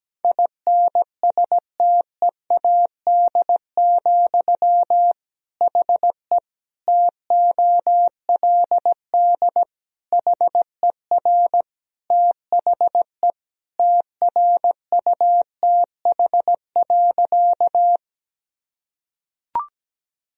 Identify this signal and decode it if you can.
Morse code, heard as a keyed tone, about 17 wpm, 700 Hz. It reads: INSTEAD, HE TOLD HER THE TRUTH.
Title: Sentences from Top 1000 Words - Review Part 4 - 17wpm